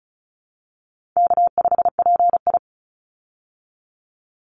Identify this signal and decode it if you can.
Morse code, heard as a keyed tone, about 34 wpm, 700 Hz. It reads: K5PI
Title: Easy Call Signs Repeated 2x - 35wpm